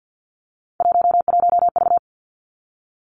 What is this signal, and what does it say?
Morse code, heard as a keyed tone, about 50 wpm, 700 Hz. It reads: 11V
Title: Three Challenging Letter-Number Combinations 50wpm